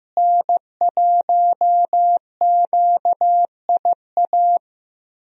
Morse code, heard as a keyed tone, about 15 wpm, 700 Hz. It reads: N1QIA
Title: USA Call Signs - 5 Characters - Repeated 3x - 15wpm